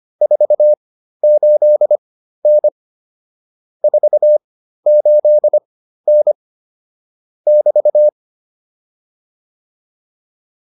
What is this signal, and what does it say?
Morse code, heard as a keyed tone, about 25 wpm, spaced as for 13 wpm, 600 Hz. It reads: 48N 48N <BT>
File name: QSO 210 13